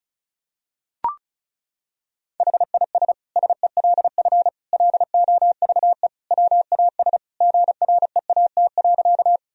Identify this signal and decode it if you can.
Morse code, heard as a keyed tone, about 36 wpm, 700 Hz. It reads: HIS SELF LOVE WAS GREAT.
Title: Sentences from Top 1000 Words - Review Part 1 - 35wpm